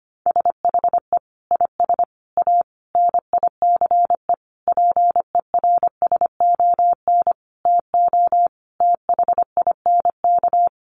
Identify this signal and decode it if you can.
Morse code, heard as a keyed tone, about 25 wpm, 700 Hz. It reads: SHE IS A NICE PERSON TO THINK
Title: Sentences from Top 1100 Words 25wpm